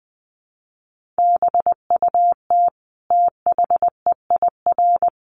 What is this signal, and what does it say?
Morse code, heard as a keyed tone, about 20 wpm, 700 Hz. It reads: BUT THEIR
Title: Sentences from Top 1400 Words 20wpm